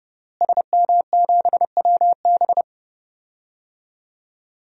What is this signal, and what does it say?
Morse code, heard as a keyed tone, about 30 wpm, 700 Hz. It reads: SM7WB
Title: DXCC-100 Call Signs - 5 Characters - Repeated 2x - 30wpm